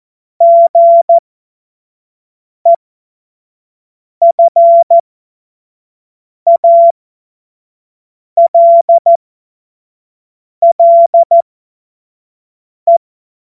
Morse code, heard as a keyed tone, about 14 wpm, spaced as for 5 wpm, 680 Hz. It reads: GEFALLE
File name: Morsekurs / Witze / Witz 06. September 2022 25 F70